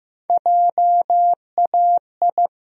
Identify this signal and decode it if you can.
Morse code, heard as a keyed tone, about 15 wpm, 700 Hz. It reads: JAI
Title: USA Call Sign Postfixes - 3 Letters 15wpm